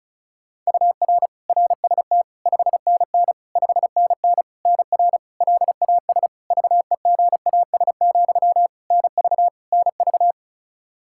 Morse code, heard as a keyed tone, about 36 wpm, 700 Hz. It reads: UR RST 5NN 5NN NR LAS VEGAS, NV NV
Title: K1USN SST QSO Practice 35wpm